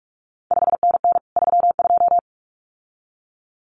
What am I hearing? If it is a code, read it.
Morse code, heard as a keyed tone, about 46 wpm, 700 Hz. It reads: 5NN 32
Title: CQ WW DX Contest Exchange - 45wpm